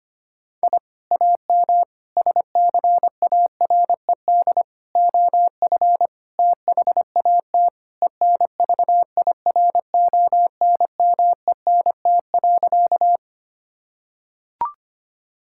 Morse code, heard as a keyed tone, about 25 wpm, 700 Hz. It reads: I AM SCARED OF THAT ENVIRONMENT.
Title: Sentences from Top 1000 Words - Review Part 5 - 25wpm